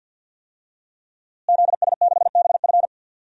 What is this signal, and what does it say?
Morse code, heard as a keyed tone, about 48 wpm, 700 Hz. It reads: ZS6BF
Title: DXCC-100 Call Signs - 5 Characters - Repeated 2x - 50wpm